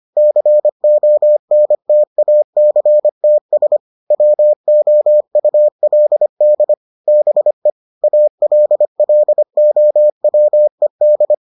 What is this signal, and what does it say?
Morse code, heard as a keyed tone, about 25 wpm, 600 Hz. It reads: CONTACTS WOULD BE ALLOWED